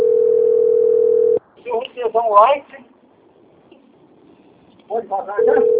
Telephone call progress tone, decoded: ringback tone

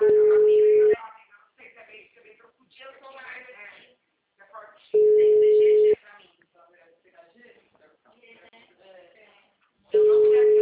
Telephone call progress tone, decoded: ringback tone

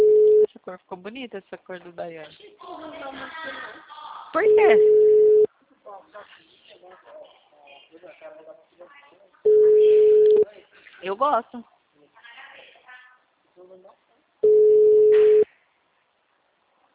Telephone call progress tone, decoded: ringback tone

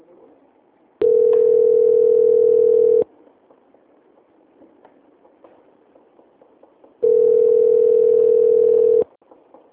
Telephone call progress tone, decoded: ringback tone

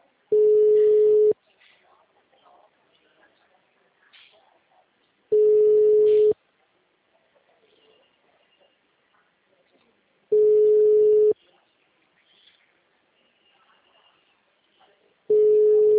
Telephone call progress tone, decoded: ringback tone

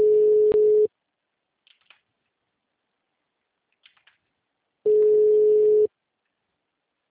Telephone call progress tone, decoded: ringback tone

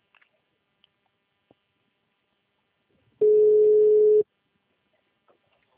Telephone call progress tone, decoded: ringback tone

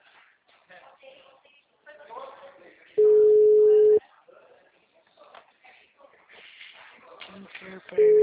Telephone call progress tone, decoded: ringback tone

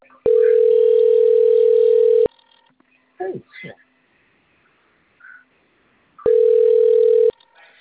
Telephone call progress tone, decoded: ringback tone

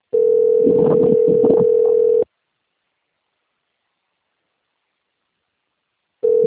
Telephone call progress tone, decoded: ringback tone

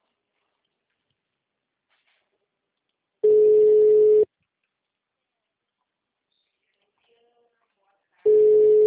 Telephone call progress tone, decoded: ringback tone

